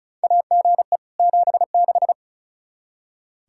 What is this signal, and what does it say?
Morse code, heard as a keyed tone, about 36 wpm, 700 Hz. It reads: AGE 76
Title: K1USN SST QSO Practice 35wpm